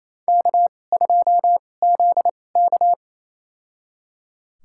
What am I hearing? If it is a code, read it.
Morse code, heard as a keyed tone, about 27 wpm, spaced as for 20 wpm, 700 Hz. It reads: K2ZK
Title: USA Call Signs - 4 Characters - 2x Character Spacing - 28wpm